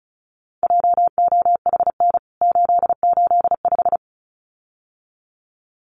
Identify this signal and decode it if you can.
Morse code, heard as a keyed tone, about 34 wpm, 700 Hz. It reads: JOHN 885
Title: CWT Exchange 35wpm